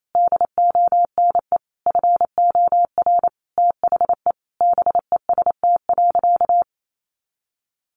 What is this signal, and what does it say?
Morse code, heard as a keyed tone, about 28 wpm, 700 Hz. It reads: DONE FOR THE BEST.
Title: Sentences from Top 600 Words 28wpm